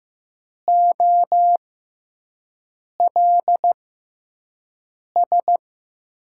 Morse code, heard as a keyed tone, about 15 wpm, spaced as for 5 wpm, 700 Hz. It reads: OLS
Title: USA Call Signs - 6x Character Spacing - 15wpm